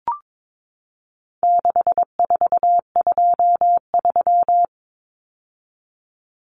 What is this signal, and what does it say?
Morse code, heard as a keyed tone, about 22 wpm, 700 Hz. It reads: 6423
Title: Numbers - 4-Digit 22wpm